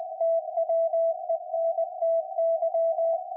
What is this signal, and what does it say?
Morse code, heard as a keyed tone, about 20 wpm, 660 Hz. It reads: TWENTY
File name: IC-7100